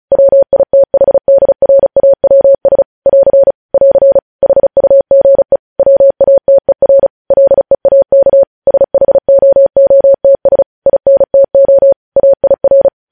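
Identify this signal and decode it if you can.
Morse code, heard as a keyed tone, about 36 wpm, 550 Hz. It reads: WITHDRAWS <AR> <AR> HUGE WATER LEAK SHOOTS INTO AIR
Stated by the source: News headlines sent in Morse at 35 wpm.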